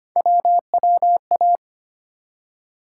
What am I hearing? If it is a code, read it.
Morse code, heard as a keyed tone, about 25 wpm, 700 Hz. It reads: WWA
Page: morse-code-ninja-practice-25wpm